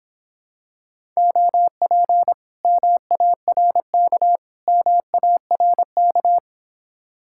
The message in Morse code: OP MARK MARK

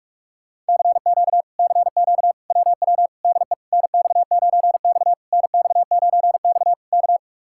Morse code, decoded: CQ CQ WW DE NX9X NX9X K

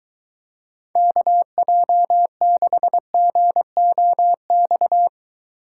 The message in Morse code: KJ6GOX

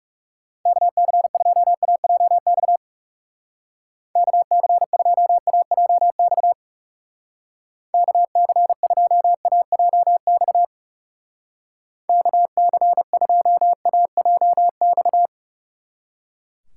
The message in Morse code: KC2AJX KC2AJX KC2AJX KC2AJX

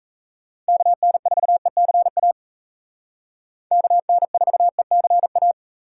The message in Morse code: KN4ECA KN4ECA

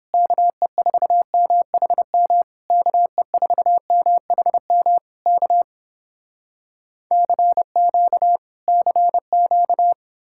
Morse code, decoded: KE4MHM KE4MHM K CQ CQ

